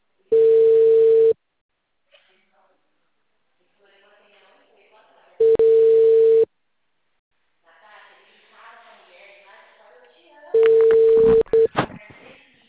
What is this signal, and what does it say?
Ringback tone (China, 450 Hz), with speech.
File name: out- -2005- - - .22